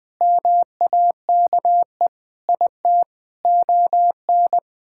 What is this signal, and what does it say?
Morse code, heard as a keyed tone, about 20 wpm, 700 Hz. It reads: MAKE IT ON
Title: Sentences from Top 1000 Words - Review Part 6 - 20wpm